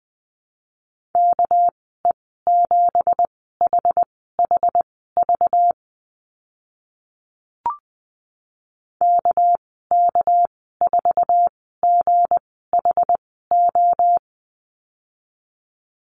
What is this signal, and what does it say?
Morse code, heard as a keyed tone, about 20 wpm, spaced as for 15 wpm, 700 Hz. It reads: KE7HHV KK4GHO
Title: 207 Lesson 33g - Letter j - Call Signs 20wpm